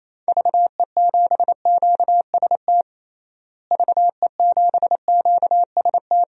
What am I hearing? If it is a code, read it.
Morse code, heard as a keyed tone, about 28 wpm, 700 Hz. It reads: VE7QST VE7QST
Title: DXCC-100 Call Signs - 6 Characters - Repeated 3x - 28wpm